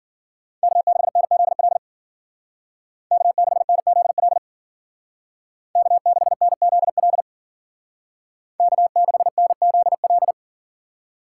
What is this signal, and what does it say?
Morse code, heard as a keyed tone, about 53 wpm, 700 Hz. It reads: K6NZL K6NZL K6NZL K6NZL